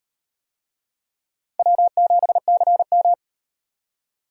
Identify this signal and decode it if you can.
Morse code, heard as a keyed tone, about 38 wpm, 700 Hz. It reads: W7CM